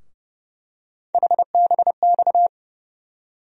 Morse code, heard as a keyed tone, about 30 wpm, 700 Hz. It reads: HBX
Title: USA Call Sign Postfixes - 3 Letters 30wpm